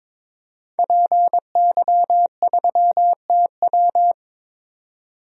Morse code, heard as a keyed tone, about 22 wpm, 700 Hz. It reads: PY3TW
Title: DXCC-100 Call Signs - 5 Characters - 22wpm